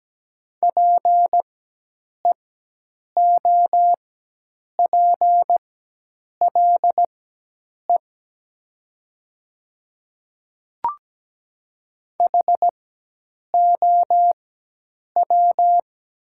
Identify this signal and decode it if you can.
Morse code, heard as a keyed tone, about 17 wpm, spaced as for 8 wpm, 700 Hz. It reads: PEOPLE HOW
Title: 4x Character Spacing - Top 100 Words - 17wpm